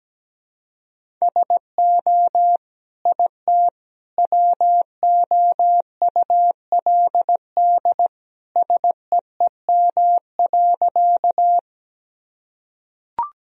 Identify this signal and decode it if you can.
Morse code, heard as a keyed tone, about 17 wpm, 700 Hz. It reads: SO IT WOULD SEEM.
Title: Sentences from Top 1000 Words - Review Part 1 - 17wpm